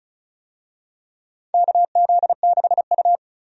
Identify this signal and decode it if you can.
Morse code, heard as a keyed tone, about 34 wpm, 700 Hz. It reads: KZ6U
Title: USA Call Signs - 4 Characters - 35wpm